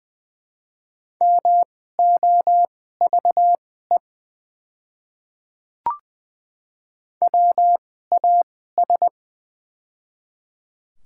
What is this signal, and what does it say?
Morse code, heard as a keyed tone, about 20 wpm, spaced as for 15 wpm, 690 Hz. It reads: MOVE WAS
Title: 2x Character Spacing - Top 200 Words - 20wpm